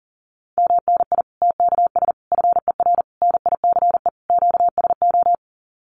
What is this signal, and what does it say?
Morse code, heard as a keyed tone, about 40 wpm, 700 Hz. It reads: MNI TKS FER NICE QSO
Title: Common QSO Phrases 40wpm